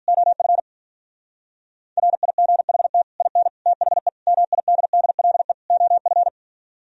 Morse code, heard as a keyed tone, about 42 wpm, 700 Hz. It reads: OF RIGHT IN THE MIDDLE OF
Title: Speed-Racing - Sets of 5 Words 35wpm